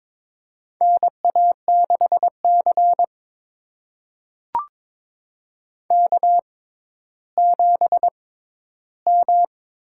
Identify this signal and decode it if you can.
Morse code, heard as a keyed tone, about 22 wpm, spaced as for 12 wpm, 700 Hz. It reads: NA6C K7M
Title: USA Call Signs - 4 Characters - 6x Character Spacing - 22wpm